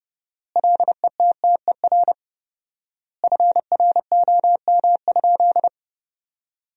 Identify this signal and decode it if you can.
Morse code, heard as a keyed tone, about 31 wpm, 700 Hz. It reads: LETTER FROM?
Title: 4x Word-Spacing - Sentences from Top 300 Words - 30wpm